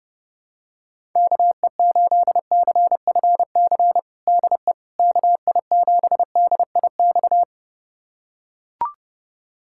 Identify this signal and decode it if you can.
Morse code, heard as a keyed tone, about 30 wpm, 700 Hz. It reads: KE8CFC DE KI7DIX
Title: USA Callsign DE Callsign - 6 Characters - 30wpm